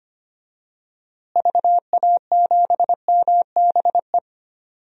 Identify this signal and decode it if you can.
Morse code, heard as a keyed tone, about 25 wpm, 700 Hz. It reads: VA7MBE
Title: DXCC-100 Call Signs - 6 Characters - Repeated 2x - 25wpm